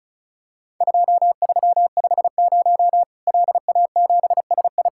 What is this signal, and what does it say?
Morse code, heard as a keyed tone, about 36 wpm, 700 Hz. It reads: 2350 LA7SI
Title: Worked All Europe DX Contest - QTC Exchange - 35wpm